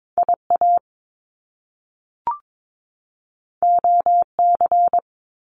Morse code, heard as a keyed tone, about 22 wpm, 700 Hz. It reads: IA OC
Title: Top Two Letter Combinations in Words 22wpm